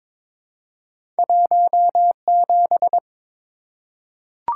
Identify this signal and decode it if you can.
Morse code, heard as a keyed tone, about 23 wpm, 700 Hz. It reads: 17
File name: practice / 22wpm / Two Letter-Number Combinations - 22wpm